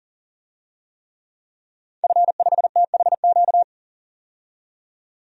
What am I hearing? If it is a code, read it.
Morse code, heard as a keyed tone, about 40 wpm, 700 Hz. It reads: F5THQ
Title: DXCC-100 Call Signs - 5 Characters - 40wpm